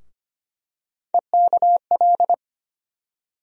Morse code, heard as a keyed tone, about 25 wpm, 700 Hz. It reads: EKL
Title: USA Call Sign Postfixes - 3 Letters 25wpm